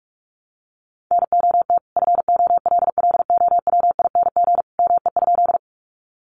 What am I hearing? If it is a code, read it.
Morse code, heard as a keyed tone, about 46 wpm, 700 Hz. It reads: NOT FOLLOWING ME?
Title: Sentences from Top 700 Words 45wpm